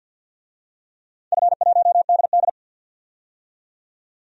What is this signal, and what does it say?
Morse code, heard as a keyed tone, about 45 wpm, 700 Hz. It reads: F1DD
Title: DXCC-100 Call Signs - 4 Characters - 50wpm